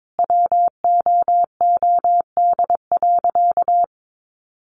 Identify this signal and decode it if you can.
Morse code, heard as a keyed tone, about 22 wpm, 700 Hz. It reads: WOOD.
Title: Sentences from Top 1500 Words 22wpm